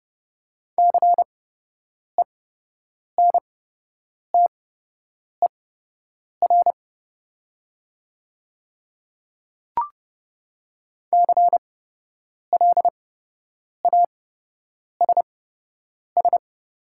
Morse code, heard as a keyed tone, about 29 wpm, spaced as for 8 wpm, 700 Hz. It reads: CENTER CLASS
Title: 8x Character Spacing - Top 500 Words - 30wpm